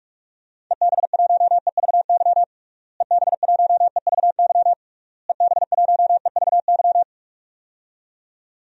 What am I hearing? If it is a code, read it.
Morse code, heard as a keyed tone, about 41 wpm, 700 Hz. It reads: EB1EVY EB1EVY EB1EVY
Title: DXCC-100 Call Signs - 6 Characters - Repeated 3x - 45wpm